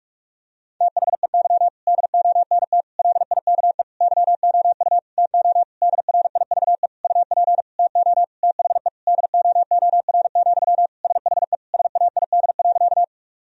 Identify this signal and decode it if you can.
Morse code, heard as a keyed tone, about 46 wpm, 700 Hz. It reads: THEY DONT LIKE YOU TO DRIVE UP TO THE DOOR, SHE SAID.
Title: Sentences from Top 1000 Words - Review Part 2 - 45wpm